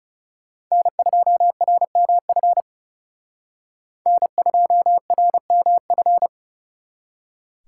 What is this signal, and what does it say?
Morse code, heard as a keyed tone, about 39 wpm, 700 Hz. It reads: N2RMF N2RMF